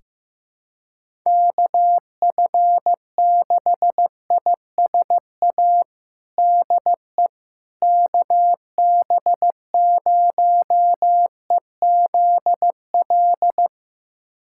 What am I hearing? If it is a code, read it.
Morse code, heard as a keyed tone, about 16 wpm, 700 Hz. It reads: KF6ISA DE KB0EZL
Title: USA Callsign DE Callsign - 6 Characters - 15wpm